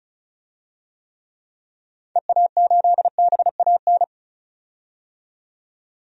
Morse code, heard as a keyed tone, about 34 wpm, 700 Hz. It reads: EA8BAN
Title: DXCC-100 Call Signs - 6 Characters - 35wpm